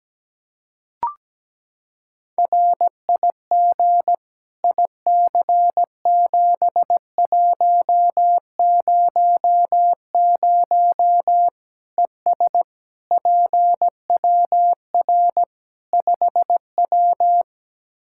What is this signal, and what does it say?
Morse code, heard as a keyed tone, about 17 wpm, 700 Hz. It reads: RIG IC7100 ES PWR 5W
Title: Practice QSO - Exchange 2 - Info - 17wpm